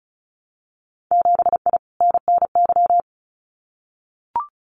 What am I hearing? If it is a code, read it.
Morse code, heard as a keyed tone, about 36 wpm, 700 Hz. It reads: 7I NNY
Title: Winter Field Day Exchanges Encore 35wpm